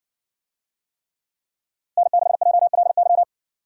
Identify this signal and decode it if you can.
Morse code, heard as a keyed tone, about 60 wpm, 700 Hz. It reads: N6JLY